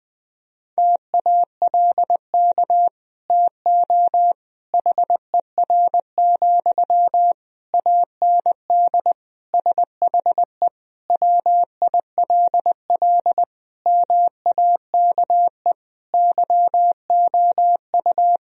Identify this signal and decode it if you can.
Morse code, heard as a keyed tone, about 21 wpm, 700 Hz. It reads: TALK TO HER, AND SHE WILL MAKE YOU
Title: Sentences from Top 1000 Words - Review Part 3 - 20wpm